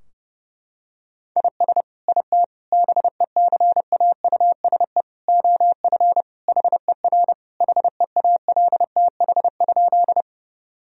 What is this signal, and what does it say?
Morse code, heard as a keyed tone, about 30 wpm, 700 Hz. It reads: IS IT BECAUSE OF HER HEALTH?
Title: Sentences from Top 1000 Words - Review Part 3 - 30wpm